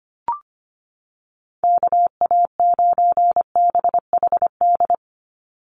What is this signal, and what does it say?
Morse code, heard as a keyed tone, about 25 wpm, 700 Hz. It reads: KA9BHD
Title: Call Signs Repeated 2x - 25wpm